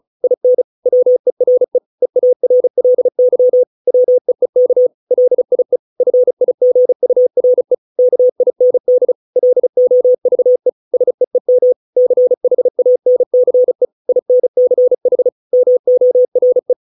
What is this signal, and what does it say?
Morse code, heard as a keyed tone, about 34 wpm, 500 Hz. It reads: IN WERE EARLY WEEK LIE FIGURE KIND LOVE SEEM CHANCE INCH MORE